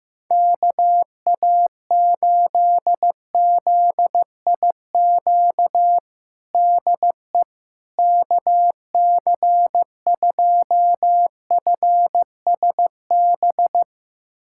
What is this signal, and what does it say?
Morse code, heard as a keyed tone, about 15 wpm, 700 Hz. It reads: KA8ZIQ DE KC2FSB
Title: USA Callsign DE Callsign 15wpm